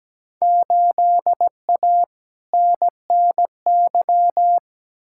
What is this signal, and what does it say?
Morse code, heard as a keyed tone, about 17 wpm, 700 Hz. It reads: 8A NNY
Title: Field Day Exchanges Encore - 17wpm